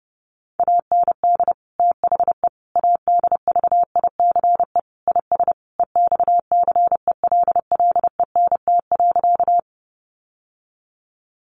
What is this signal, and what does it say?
Morse code, heard as a keyed tone, about 30 wpm, 700 Hz. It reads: AND THE ADVICE IS EXCELLENT.
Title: Sentences from Top 1100 Words 30wpm